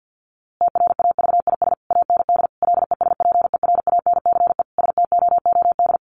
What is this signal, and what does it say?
Morse code, heard as a keyed tone, about 49 wpm, 700 Hz. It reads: TRAVIS AND LESPERANCE STOOD
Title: A Sound Of Thunder 50wpm